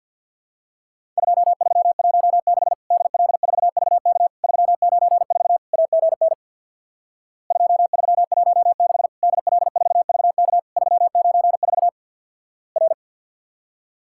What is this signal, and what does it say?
Morse code, heard as a keyed tone, about 50 wpm, 700 Hz. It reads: 2316 DL4VK 394 AGN 2316 DL4VK 394 R
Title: Worked All Europe DX Contest - QTC Exchange with Repeats - 50wpm